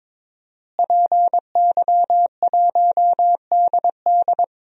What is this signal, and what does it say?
Morse code, heard as a keyed tone, about 22 wpm, 700 Hz. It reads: PY1DD
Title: DXCC-100 Call Signs - 5 Characters - Repeated 3x - 22wpm